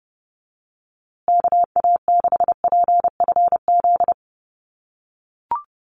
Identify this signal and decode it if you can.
Morse code, heard as a keyed tone, about 30 wpm, 700 Hz. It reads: KA6PFZ
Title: USA Call Signs Repeated 2x 30wpm